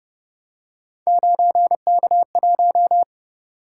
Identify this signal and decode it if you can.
Morse code, heard as a keyed tone, about 29 wpm, 700 Hz. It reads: 9K1